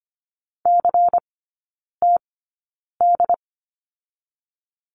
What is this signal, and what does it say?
Morse code, heard as a keyed tone, about 25 wpm, spaced as for 9 wpm, 700 Hz. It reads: CTD
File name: course / 25:10wpm / 093 Lesson 16e - Letter c - Three Characters 25wpm